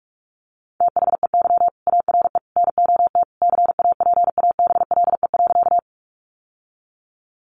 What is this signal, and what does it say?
Morse code, heard as a keyed tone, about 44 wpm, 700 Hz. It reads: THEY ARE NOT CAPABLE.
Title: Sentences from Top 1200 Words 45wpm